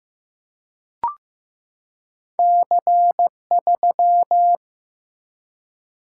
Morse code, heard as a keyed tone, about 15 wpm, 700 Hz. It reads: C3
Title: Single-Letter Single-Number 15wpm